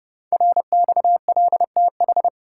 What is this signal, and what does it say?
Morse code, heard as a keyed tone, about 30 wpm, 700 Hz. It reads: RXLTH
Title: Code Groups - 5 Letters - 30wpm